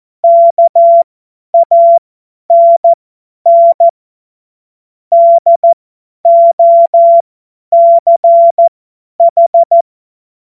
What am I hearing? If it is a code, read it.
Morse code, heard as a keyed tone, about 14 wpm, spaced as for 10 wpm, 680 Hz. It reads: KANN DOCH
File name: Morsekurs / Witze / Witz 06 Februar 2024 50 F70